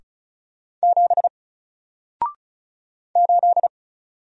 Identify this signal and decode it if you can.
Morse code, heard as a keyed tone, about 36 wpm, 700 Hz. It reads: 7 8
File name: Single-Letter-Number-35wpm